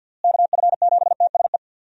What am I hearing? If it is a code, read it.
Morse code, heard as a keyed tone, about 52 wpm, 700 Hz. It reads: KF7TSE
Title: USA Call Signs - 6 Characters - Repeated 3x - 50wpm